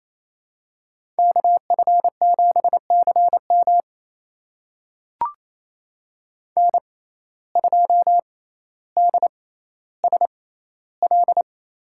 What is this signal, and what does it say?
Morse code, heard as a keyed tone, about 28 wpm, spaced as for 14 wpm, 700 Hz. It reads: KF7CM N2DSL
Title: USA Call Signs - 5 Characters - 6x Character Spacing - 28wpm